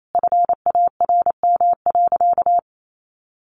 Morse code, heard as a keyed tone, about 28 wpm, 700 Hz. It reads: FARM.
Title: Sentences from Top 1200 Words 28wpm